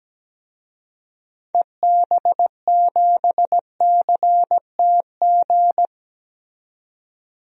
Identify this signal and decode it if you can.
Morse code, heard as a keyed tone, about 17 wpm, 700 Hz. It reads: EB7CTG
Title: DXCC-100 Call Signs - 6 Characters - 17wpm